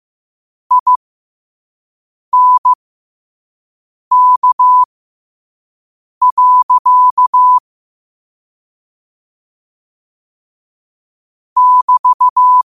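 Morse code, heard as a keyed tone, about 15 wpm, spaced as for 5 wpm, 1000 Hz. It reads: INK. <BT>
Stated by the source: Morse code sent at 5 wpm.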